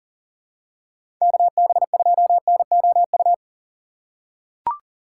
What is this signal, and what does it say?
Morse code, heard as a keyed tone, about 39 wpm, 700 Hz. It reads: KB2NOU
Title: USA Call Signs - 6 Characters - 4x Character Spacing - 40wpm